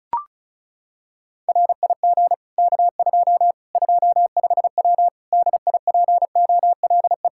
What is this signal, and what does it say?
Morse code, heard as a keyed tone, about 36 wpm, 700 Hz. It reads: RIG K2 25W DIPOLE
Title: Practice QSO - Exchange 2 - Info - 35wpm